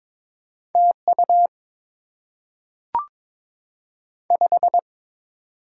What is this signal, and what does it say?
Morse code, heard as a keyed tone, about 22 wpm, 690 Hz. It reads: TU 5
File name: Top 20 QSO Elements - 6x Character Spacing - 22wpm